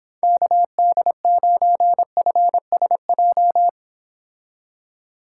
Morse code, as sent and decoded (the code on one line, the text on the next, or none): -.- -.. ----. ..-. ... .---
KD9FSJ